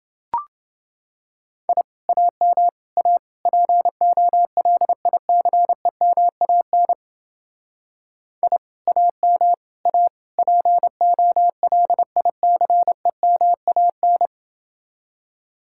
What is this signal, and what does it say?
Morse code, heard as a keyed tone, about 30 wpm, 700 Hz. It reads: I AM A POLICEMAN I AM A POLICEMAN